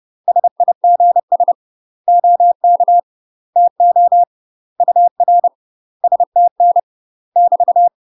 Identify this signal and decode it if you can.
Morse code, heard as a keyed tone, about 30 wpm, 700 Hz. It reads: SIGS OK TO UR STN <BT>
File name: UK MASTER QSOs TRAINING / 30WPM / UK MASTER QSOs 0018